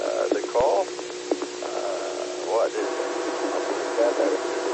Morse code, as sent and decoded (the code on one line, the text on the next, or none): none